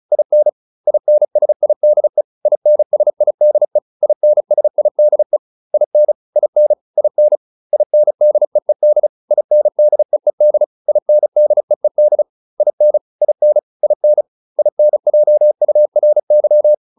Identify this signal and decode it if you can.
Morse code, heard as a keyed tone, about 34 wpm, 600 Hz. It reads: IN INSIDE INSIDE INSIDE IN IN IN INDEED INDEED INDEED IN IN IN INJURY
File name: IN6 35